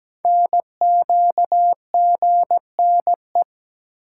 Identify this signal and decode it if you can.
Morse code, heard as a keyed tone, about 17 wpm, 700 Hz. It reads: NQGNE
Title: Code Groups - 5 Letters - 17wpm